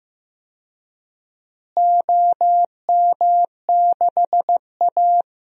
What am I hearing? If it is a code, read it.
Morse code, heard as a keyed tone, about 15 wpm, 700 Hz. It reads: OM6A
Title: DXCC-100 Call Signs - 4 Characters - Repeated 3x - 15wpm